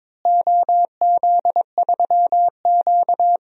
Morse code, heard as a keyed tone, about 22 wpm, 700 Hz. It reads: OZ3Q
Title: DXCC-100 Call Signs - 4 Characters - Repeated 3x - 22wpm